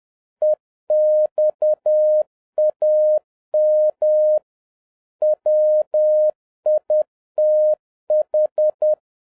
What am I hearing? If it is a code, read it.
Morse code, heard as a keyed tone, about 10 wpm, 610 Hz. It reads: EXAM WITH